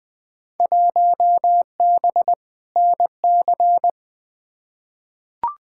Morse code, heard as a keyed tone, about 20 wpm, 700 Hz. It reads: 1B NC
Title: Field Day Exchanges - 20wpm